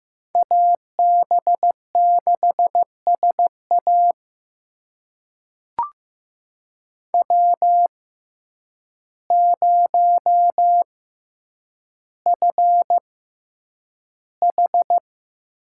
Morse code, heard as a keyed tone, about 15 wpm, spaced as for 8 wpm, 700 Hz. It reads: AB6SA W0FH
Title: USA Call Signs - 5 Characters - 6x Character Spacing - 15wpm